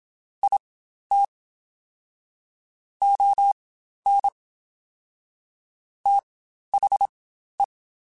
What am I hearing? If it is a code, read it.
Morse code, heard as a keyed tone, about 27 wpm, spaced as for 12 wpm, 800 Hz. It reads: IT ON THE
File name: Spaced / Chuckles Spaced-mp3s / HumanErr sp @25